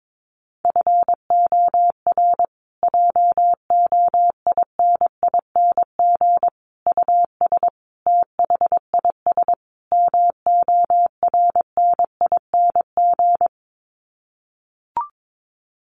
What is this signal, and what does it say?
Morse code, heard as a keyed tone, about 22 wpm, 700 Hz. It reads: FOR JOINING US THIS MORNING
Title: Sets of 5 Words Encore 22wpm